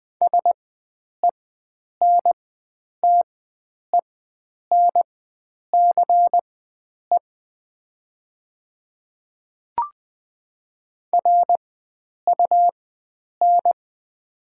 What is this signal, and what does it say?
Morse code, heard as a keyed tone, about 20 wpm, spaced as for 9 wpm, 700 Hz. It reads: SENTENCE RUN
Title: 4x Character Spacing - Top 300 Words - 20wpm